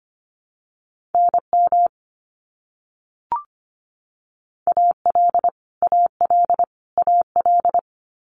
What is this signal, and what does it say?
Morse code, heard as a keyed tone, about 25 wpm, 700 Hz. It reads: NM AL AL AL
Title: US States Repeated 3x 25wpm